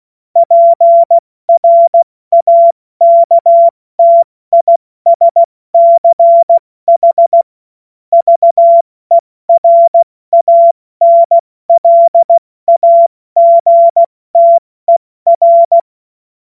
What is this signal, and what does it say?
Morse code, heard as a keyed tone, about 16 wpm, 680 Hz. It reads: PRAKTISCH VERANLAGTER
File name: Morsekurs / Witze / Witz 07. Dezember 2021, 70 F80